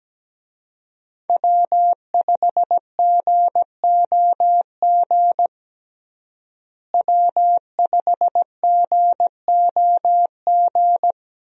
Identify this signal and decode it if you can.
Morse code, heard as a keyed tone, about 17 wpm, 700 Hz. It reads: W5GOG W5GOG
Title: USA Call Signs - 5 Characters - Repeated 2x - 17wpm